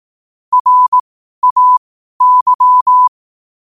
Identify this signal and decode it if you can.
Morse code, heard as a keyed tone, about 18 wpm, spaced as for 12 wpm, 1000 Hz. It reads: RAY
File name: qotdfiles / 12wpm